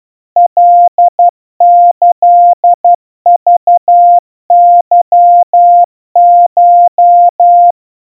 Morse code (Japanese Wabun code), heard as a keyed tone, about 12 wpm, 700 Hz. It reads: カキクケコ
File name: aiueo 40 cpm 700 Hz Crispy